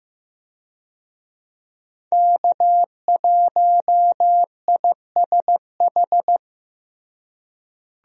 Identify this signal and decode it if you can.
Morse code, heard as a keyed tone, about 15 wpm, 700 Hz. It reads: K1ISH
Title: USA Call Signs - 5 Characters - Repeated 3x - 15wpm